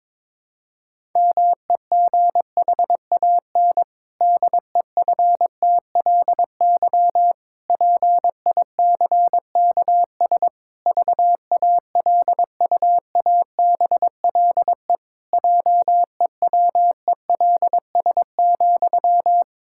Morse code, heard as a keyed tone, about 22 wpm, 700 Hz. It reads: MEGHAN DEFTLY PICKS VALUABLE JEWELS,
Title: Pangram Challenge v2 - 22wpm